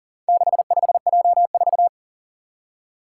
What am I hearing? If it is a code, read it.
Morse code, heard as a keyed tone, about 40 wpm, 700 Hz. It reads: 65J4
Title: Four Challenging Letter-Number Combinations 40wpm